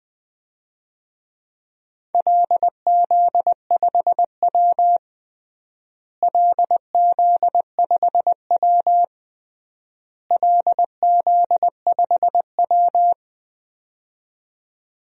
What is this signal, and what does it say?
Morse code, heard as a keyed tone, about 20 wpm, 700 Hz. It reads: LZ5W LZ5W LZ5W
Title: DXCC-100 Call Signs - 4 Characters - Repeated 3x - 20wpm